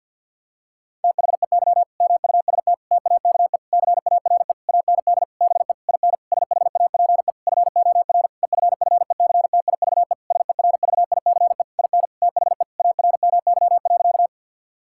Morse code, heard as a keyed tone, about 50 wpm, 700 Hz. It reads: THEY MUST TAKE CARE AND BE IN SHAPE FOR EFFECTIVE SERVICE IN THE ARMY.
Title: Sentences from Top 1000 Words - Review Part 6 - 50wpm